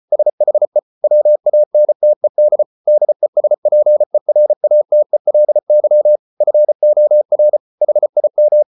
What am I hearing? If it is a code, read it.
Morse code, heard as a keyed tone, about 34 wpm, 600 Hz. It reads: SHE WANTED DESPERATELY FOR HIM